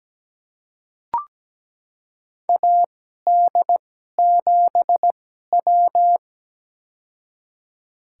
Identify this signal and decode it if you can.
Morse code, heard as a keyed tone, about 17 wpm, spaced as for 12 wpm, 700 Hz. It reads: AD7W
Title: USA Call Signs - 2x Character Spacing - 17wpm